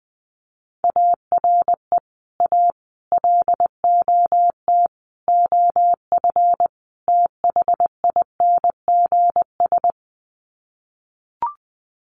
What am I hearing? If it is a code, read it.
Morse code, heard as a keyed tone, about 20 wpm, 700 Hz. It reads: ARE A LOT OF THINGS
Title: Sets of 5 Words Encore2 20wpm